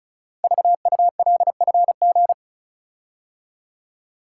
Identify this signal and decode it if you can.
Morse code, heard as a keyed tone, about 36 wpm, 700 Hz. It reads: VULFG
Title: Code Groups - 5 Letters - 35wpm